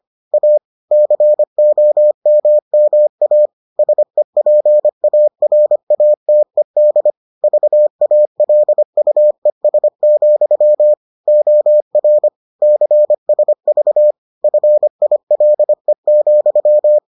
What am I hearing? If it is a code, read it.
Morse code, heard as a keyed tone, about 25 wpm, 600 Hz. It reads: A COMMA SEPARATED VALUES, OR CSV FILE,